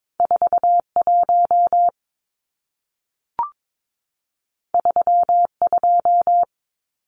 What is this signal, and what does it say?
Morse code, heard as a keyed tone, about 22 wpm, 700 Hz. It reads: 41 32
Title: Numbers - 2-Digit 22wpm